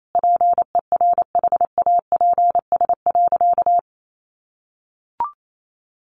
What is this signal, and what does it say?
Morse code, heard as a keyed tone, about 28 wpm, 700 Hz. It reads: PERHAPS.
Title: Sentences from Top 1400 Words 28wpm